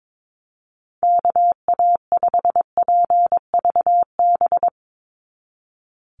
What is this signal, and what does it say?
Morse code, heard as a keyed tone, about 22 wpm, 700 Hz. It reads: KA5PVB
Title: NA SOTA Chaser Callsigns 22wpm